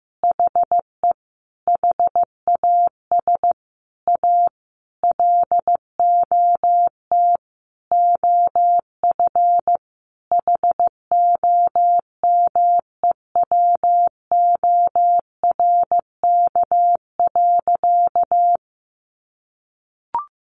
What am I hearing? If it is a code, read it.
Morse code, heard as a keyed tone, about 15 wpm, 700 Hz. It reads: HE HAS A LOT OF HOMEWORK.
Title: Additional Easy Phrases 15wpm